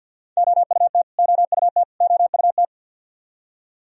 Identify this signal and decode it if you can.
Morse code, heard as a keyed tone, about 51 wpm, 700 Hz. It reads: OUT OUT OUT
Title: Top 100 Words Repeated 50wpm